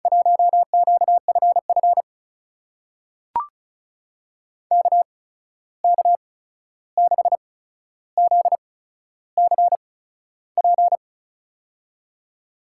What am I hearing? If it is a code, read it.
Morse code, heard as a keyed tone, about 36 wpm, spaced as for 13 wpm, 700 Hz. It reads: 1QFF KK6ZCP
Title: USA Call Signs - 6 Characters - 8x Character Spacing - 35wpm